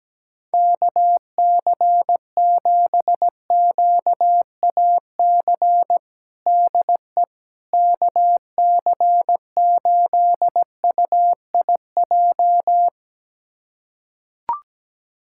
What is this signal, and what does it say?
Morse code, heard as a keyed tone, about 17 wpm, 700 Hz. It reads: KC7QAC DE KC8UIJ
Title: USA Callsign DE Callsign - 6 Characters - 17wpm